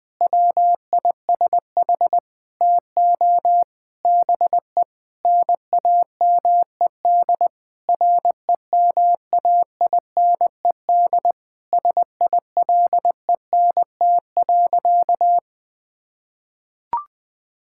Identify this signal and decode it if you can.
Morse code, heard as a keyed tone, about 21 wpm, 700 Hz. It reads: WISH TO BE NAMED REMAINED SILENT.
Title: Sentences with 7-Word Phrases - Spoken Beginning - 20wpm